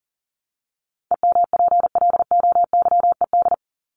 Morse code, heard as a keyed tone, about 40 wpm, 700 Hz. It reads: EMPLOYED
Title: ED Suffix - Top 100 Words with Base Words 40wpm